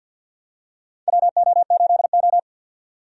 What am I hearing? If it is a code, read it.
Morse code, heard as a keyed tone, about 53 wpm, 700 Hz. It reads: WO8O
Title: USA Call Signs - 4 Characters - 50wpm